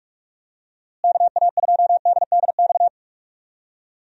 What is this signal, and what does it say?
Morse code, heard as a keyed tone, about 45 wpm, 700 Hz. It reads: KA2DDX
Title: USA Call Signs - 6 Characters - Repeated 2x - 45wpm